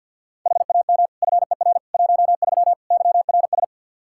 Morse code, heard as a keyed tone, about 51 wpm, 700 Hz. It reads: HAM FER 13 YRS
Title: Practice QSO - Exchange 3 - More Info - 50wpm